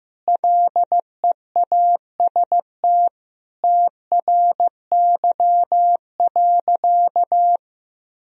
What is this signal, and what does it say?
Morse code, heard as a keyed tone, about 15 wpm, 700 Hz. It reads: LEAST TRY.
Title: Sentences from Top 1000 Words - Review Part 1 - 15wpm